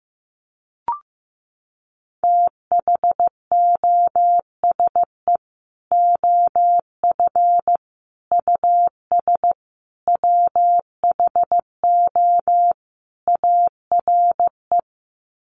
Morse code, heard as a keyed tone, about 15 wpm, 700 Hz. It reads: THOSE OF US WHO ARE
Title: Sets of 5 Words Encore2 15wpm